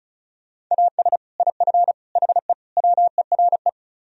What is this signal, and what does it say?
Morse code, heard as a keyed tone, about 43 wpm, 700 Hz. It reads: AS IF HE WERE
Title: Speed-Racing - Sets of 4 Words 35wpm